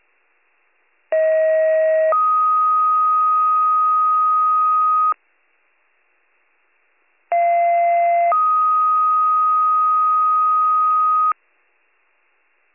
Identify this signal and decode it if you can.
Two-tone page, heard as a one-second tone, then a three-second tone, about 640 then 1250 Hz, about 690 then 1250 Hz. It is heard from 1.1 to 11.3 s.